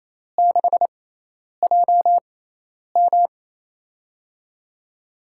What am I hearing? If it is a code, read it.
Morse code, heard as a keyed tone, about 28 wpm, spaced as for 10 wpm, 700 Hz. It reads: 6JM
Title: USA Call Signs - 6x Character Spacing - 28wpm